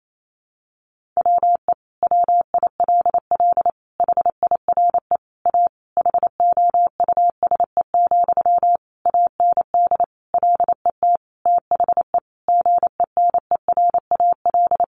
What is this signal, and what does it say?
Morse code, heard as a keyed tone, about 28 wpm, 700 Hz. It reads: WE WILL HIRE A HOUSE, AND LET THE GENERAL
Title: Sentences from Top 1400 Words 28wpm